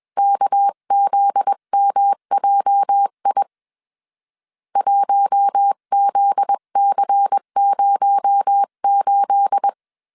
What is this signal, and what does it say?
Morse code, heard as a keyed tone, about 21 wpm, 800 Hz. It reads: KZMJI 1ZC08
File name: CW / DET / 1 9 1500C S B 20wpm 2313